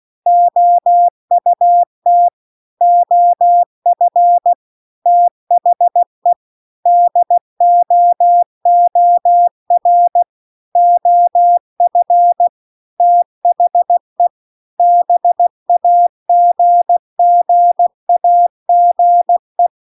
Morse code, heard as a keyed tone, about 16 wpm, 700 Hz. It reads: OUT OF THE DOOR OF THE BAGGAGE
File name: CW-The-Big-Twoheaded-River-1-16WPM